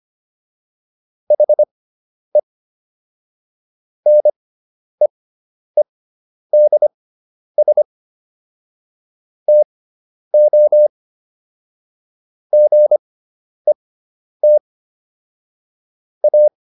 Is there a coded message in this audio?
Yes: Morse code from 1.3 to 16.5 s, heard as a keyed tone, about 25 wpm, spaced as for 10 wpm, 600 Hz. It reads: HE NEEDS TO GET A